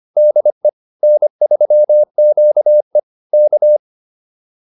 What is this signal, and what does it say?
Morse code, heard as a keyed tone, about 25 wpm, 600 Hz. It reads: DE N3QE K
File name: QSO101 25